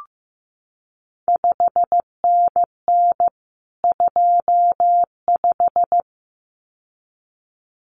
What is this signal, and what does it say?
Morse code, heard as a keyed tone, about 15 wpm, 700 Hz. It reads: 5NN 25
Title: CQ WW DX Contest Exchange Encore - 15wpm